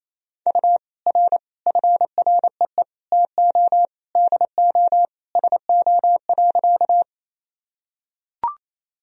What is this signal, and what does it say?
Morse code, heard as a keyed tone, about 28 wpm, 700 Hz. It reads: U R FREE TO DO SO.
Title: Sentences from Top 500 Words and Abbreviations 28wpm